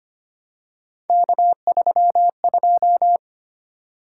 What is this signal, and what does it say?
Morse code, heard as a keyed tone, about 25 wpm, 700 Hz. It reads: K32
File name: Three Letter-Number Combinations - 25wpm